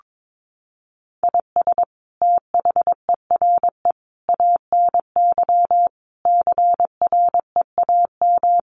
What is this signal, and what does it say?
Morse code, heard as a keyed tone, about 22 wpm, 700 Hz. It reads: IS THERE ANY CREAM
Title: Sentences from Top 1500 Words 22wpm